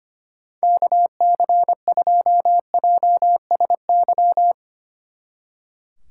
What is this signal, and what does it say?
Morse code, heard as a keyed tone, about 29 wpm, 700 Hz. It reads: KC2JSY
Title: Speed Racing - USA Call Signs 25wpm